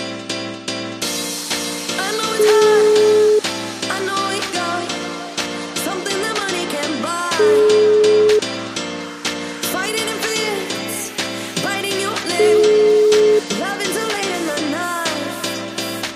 Ringback tone (Europe, 425 Hz), with noise.